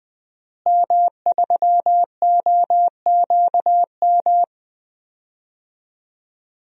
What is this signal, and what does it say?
Morse code, heard as a keyed tone, about 20 wpm, 700 Hz. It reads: M3OQM
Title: DXCC-100 Call Signs - 5 Characters - Repeated 3x - 20wpm